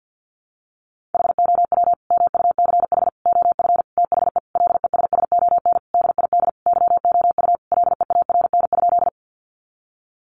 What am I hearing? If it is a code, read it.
Morse code, heard as a keyed tone, about 50 wpm, 700 Hz. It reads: HOW MUCH OF THE LESSON DID YOU LEARN?
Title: Sentences from Top 1200 Words 50wpm